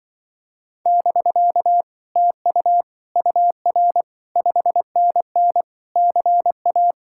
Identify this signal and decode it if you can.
Morse code, heard as a keyed tone, about 24 wpm, 700 Hz. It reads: <BK> TU UR 5NN CA
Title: Speed-Racing Common QSO Phrases - 20wpm